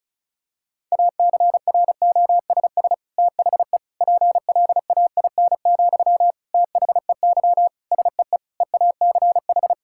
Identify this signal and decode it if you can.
Morse code, heard as a keyed tone, about 34 wpm, 690 Hz. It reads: ACROSS THE PLAIN, THEY SEE EACH
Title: Sentences from Top 1000 Words - Review Part 5 - 35wpm